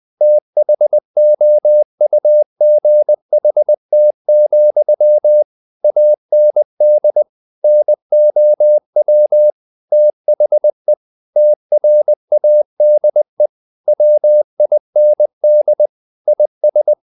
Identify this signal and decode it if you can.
Morse code, heard as a keyed tone, about 20 wpm, 600 Hz. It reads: THOUGHT, AND NOW THE TRADE WIND IS
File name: the old man and the sea part 2 20wpm